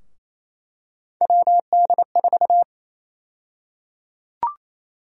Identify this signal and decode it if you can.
Morse code, heard as a keyed tone, about 28 wpm, 700 Hz. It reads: WD4
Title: USA Call Sign Prefixes - 28wpm